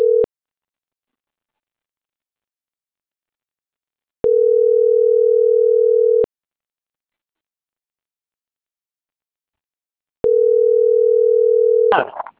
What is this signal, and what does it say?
Ringback tone (North America), 440 + 480 Hz.